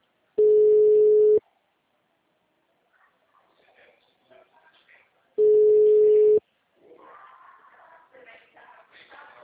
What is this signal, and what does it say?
Ringback tone (Europe, 425 Hz).
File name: out- -2025- - - .51